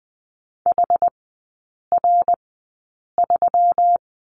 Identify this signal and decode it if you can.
Morse code, heard as a keyed tone, about 20 wpm, spaced as for 8 wpm, 700 Hz. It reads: HR3
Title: 121 Lesson 20e - Number 3 - Three Characters 20wpm